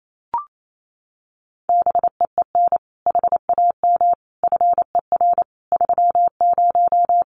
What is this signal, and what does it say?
Morse code, heard as a keyed tone, about 28 wpm, 700 Hz. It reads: BEEN HAM FER 30
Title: Common QSO Phrases Encore 28wpm